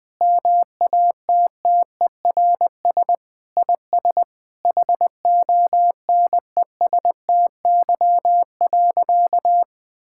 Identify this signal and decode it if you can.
Morse code, heard as a keyed tone, about 20 wpm, 700 Hz. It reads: MATTERS IS HONESTY.
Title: Sentences with 5-Word Phrases - Spoken Beginning - 20wpm